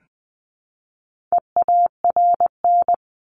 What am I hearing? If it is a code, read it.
Morse code, heard as a keyed tone, about 20 wpm, 700 Hz. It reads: EARN
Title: Top 1251-1500 Words 20wpm